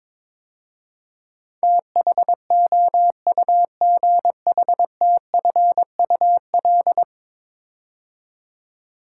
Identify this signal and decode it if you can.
Morse code, heard as a keyed tone, about 22 wpm, 700 Hz. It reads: THOUGHTFUL
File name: practice / 22wpm / 10 Letter Words - First 3 Letters Given - 22wpm